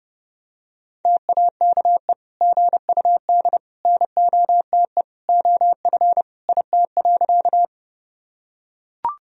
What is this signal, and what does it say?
Morse code, heard as a keyed tone, about 32 wpm, 700 Hz. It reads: TAKE GUD NOTE OF IT.
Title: Sentences from Top 500 Words and Abbreviations 30wpm